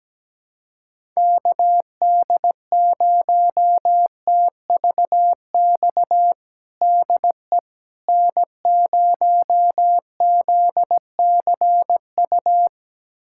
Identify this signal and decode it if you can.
Morse code, heard as a keyed tone, about 17 wpm, 700 Hz. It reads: KD0TVX DE N0ZCU
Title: USA Callsign DE Callsign 17wpm